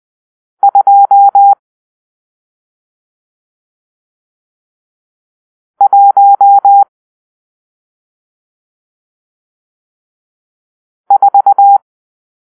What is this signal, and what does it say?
Morse code, heard as a keyed tone, about 20 wpm, 800 Hz. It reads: 2 1 4
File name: arrl / Grp-6 Drill-1